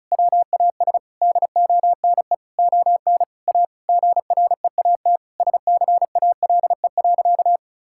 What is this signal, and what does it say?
Morse code, heard as a keyed tone, about 36 wpm, 700 Hz. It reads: WAS DONE ON A GREAT SCALE.
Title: Sentences from Top 1000 Words - Review Part 5 - 35wpm